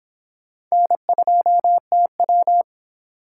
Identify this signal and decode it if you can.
Morse code, heard as a keyed tone, about 24 wpm, 700 Hz. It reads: N2TW